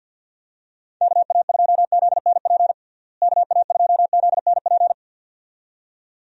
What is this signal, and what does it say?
Morse code, heard as a keyed tone, about 51 wpm, 700 Hz. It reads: KA2ZNP KA2ZNP
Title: USA Call Signs - 6 Characters - Repeated 2x - 50wpm